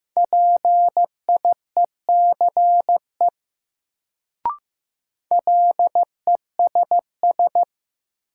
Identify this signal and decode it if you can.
Morse code, heard as a keyed tone, about 15 wpm, 700 Hz. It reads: PIECE LESS
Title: Sentences from Top 400 Words - New Word Drill - 15wpm